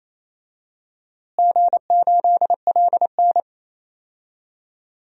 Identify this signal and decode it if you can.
Morse code, heard as a keyed tone, about 28 wpm, 700 Hz. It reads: G8LN
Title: DXCC-100 Call Signs - 4 Characters - Repeated 2x - 28wpm